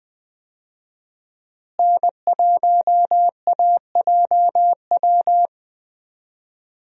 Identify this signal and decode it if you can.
Morse code, heard as a keyed tone, about 21 wpm, 700 Hz. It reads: N1AJW
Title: USA Call Signs - 5 Characters - Repeated 2x - 20wpm